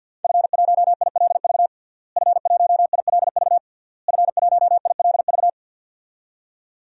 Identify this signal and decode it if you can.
Morse code, heard as a keyed tone, about 52 wpm, 700 Hz. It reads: F1ILV F1ILV F1ILV
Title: DXCC-100 Call Signs - 5 Characters - Repeated 3x - 50wpm